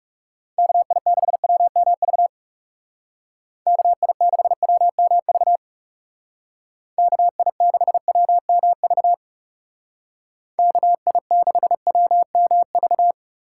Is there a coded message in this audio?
Yes: Morse code from 0.6 to 13.1 s, heard as a keyed tone, about 40 wpm, 700 Hz. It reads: KI6WMV KI6WMV KI6WMV KI6WMV